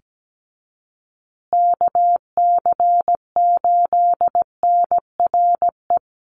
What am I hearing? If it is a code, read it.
Morse code, heard as a keyed tone, about 17 wpm, 700 Hz. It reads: KC8NRE
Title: USA Call Signs Repeated 3x 17wpm